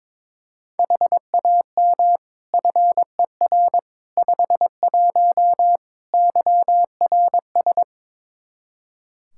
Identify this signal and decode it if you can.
Morse code, heard as a keyed tone, about 21 wpm, 700 Hz. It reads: HAM FER 51 YRS
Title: Practice QSO - Exchange 3 - More Info - 22wpm